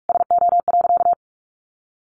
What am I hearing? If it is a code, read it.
Morse code, heard as a keyed tone, about 44 wpm, 700 Hz. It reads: SO.
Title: Sentences from Top 1100 Words 45wpm